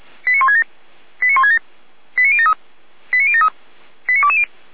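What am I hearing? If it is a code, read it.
Five-tone selective call (ZVEI), heard as quick runs of five tones, tones of about 70 ms. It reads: 89167 89167 89073 89073 89220
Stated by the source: ZVEI five-tone call received by SDR.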